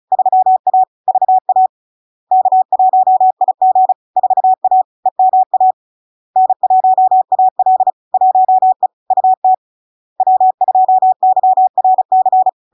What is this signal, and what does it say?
Morse code, heard as a keyed tone, about 36 wpm, 750 Hz. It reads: 3A VA K1IG 4A EMA N1AL 1E UT W2YRC
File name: LARK-FD-2022-Log-in-CW-35wpm-750-Hz